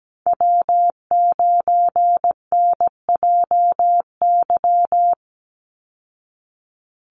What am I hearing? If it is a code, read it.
Morse code, heard as a keyed tone, about 17 wpm, 700 Hz. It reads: W9NJY
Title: Call Signs Repeated 3x - 17wpm